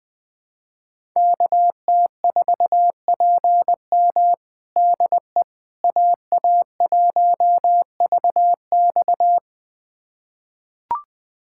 Morse code, heard as a keyed tone, about 20 wpm, 700 Hz. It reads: KT4PM DE AA1VX
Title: USA Callsign DE Callsign - 5 Characters - 20wpm